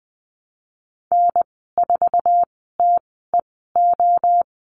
Morse code, heard as a keyed tone, about 20 wpm, spaced as for 14 wpm, 700 Hz. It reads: N4TEO
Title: 069 Lesson 12g - Letter d - Call Signs 20wpm